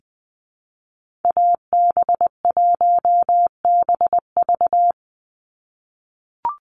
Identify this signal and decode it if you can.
Morse code, heard as a keyed tone, about 20 wpm, 700 Hz. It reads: AB1BV
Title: USA Call Signs 20wpm